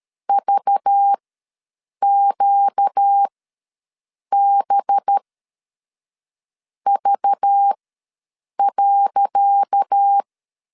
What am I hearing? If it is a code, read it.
Morse code, heard as a keyed tone, about 13 wpm, spaced as for 6 wpm, 800 Hz. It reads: VQBV.
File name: LEZIONI / 10 SIGNS / 10C DOT ws 12wpm 3548